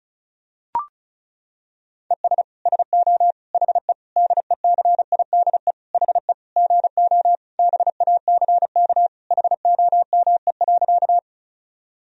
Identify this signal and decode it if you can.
Morse code, heard as a keyed tone, about 34 wpm, 700 Hz. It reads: ES SO HE DECIDE HE GO BACK HOME.
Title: Sentences from Top 500 Words and Abbreviations 35wpm